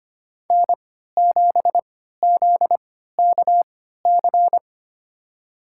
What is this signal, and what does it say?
Morse code, heard as a keyed tone, about 25 wpm, spaced as for 14 wpm, 700 Hz. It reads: N7ZKC
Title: USA Call Signs - 3x Character Spacing - 25wpm